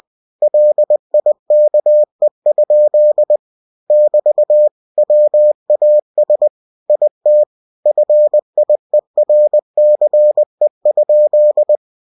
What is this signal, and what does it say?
Morse code, heard as a keyed tone, about 20 wpm, 600 Hz. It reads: LIKE? <BT> WAS IT FIERCE?
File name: cw-books / winnie-the-pooh / winnie-20wpm-page-0040